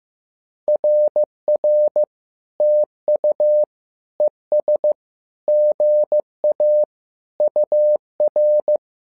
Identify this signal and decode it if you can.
Morse code, heard as a keyed tone, about 15 wpm, 600 Hz. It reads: RR TU ES GA UR
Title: SOTA QSO Practice 15wpm